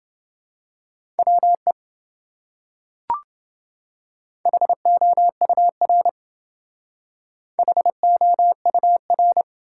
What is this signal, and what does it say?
Morse code, heard as a keyed tone, about 28 wpm, 700 Hz. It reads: WE HOUR HOUR